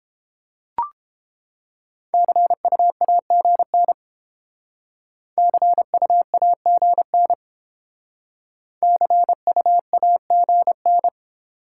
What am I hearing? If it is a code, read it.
Morse code, heard as a keyed tone, about 29 wpm, 700 Hz. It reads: CUAGN CUAGN CUAGN